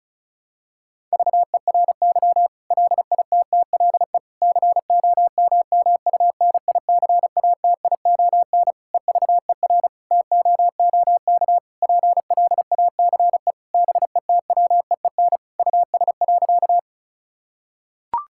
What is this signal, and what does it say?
Morse code, heard as a keyed tone, about 34 wpm, 700 Hz. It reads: VERY LITTLE COMMUNICATION EVER TOOK PLACE BETWEEN US.
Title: Sentences from Top 1000 Words - Review Part 6 - 35wpm